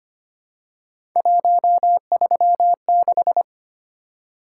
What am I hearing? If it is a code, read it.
Morse code, heard as a keyed tone, about 25 wpm, 700 Hz. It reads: 136
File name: Three Letter-Number Combinations - 25wpm